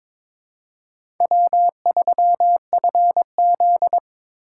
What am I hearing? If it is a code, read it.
Morse code, heard as a keyed tone, about 22 wpm, 700 Hz. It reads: W3FZ
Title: USA Call Signs - 4 Characters - 4x Character Spacing - 22wpm